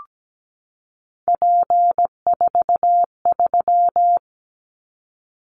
Morse code, heard as a keyed tone, about 17 wpm, 700 Hz. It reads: P43
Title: Callsign Prefixes for Top 100 DXCC - 17wpm